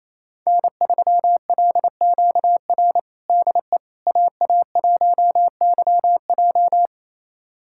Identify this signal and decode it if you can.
Morse code, heard as a keyed tone, about 27 wpm, 700 Hz. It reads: N3LQR DE AA1YJ
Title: USA Callsign DE Callsign - 5 Characters - 28wpm